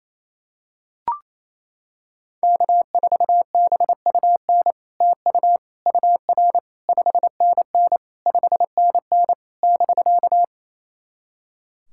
Morse code, heard as a keyed tone, about 28 wpm, 700 Hz. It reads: K4BUN TU UR 5NN 5NN <BK>
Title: SOTA QSO Practice 28wpm